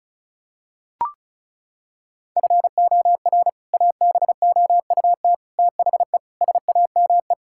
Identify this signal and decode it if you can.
Morse code, heard as a keyed tone, about 36 wpm, 700 Hz. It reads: FOR ABOUT THE SAME
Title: Sentences from Top 1000 Words - Review Part 5 - 35wpm